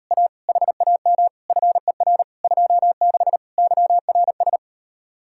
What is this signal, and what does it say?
Morse code, heard as a keyed tone, about 38 wpm, 700 Hz. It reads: A HAM FER 26 YRS